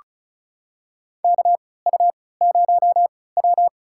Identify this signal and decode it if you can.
Morse code, heard as a keyed tone, about 36 wpm, spaced as for 20 wpm, 700 Hz. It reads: KU0W
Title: USA Call Signs - 4 Characters - 3x Character Spacing - 35wpm